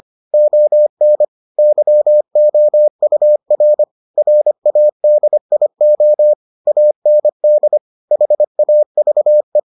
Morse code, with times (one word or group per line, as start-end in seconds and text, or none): 0.3-1.2 ON
1.6-3.8 YOUR
4.2-6.3 RADIO
6.7-7.8 AND
8.1-9.6 HAVE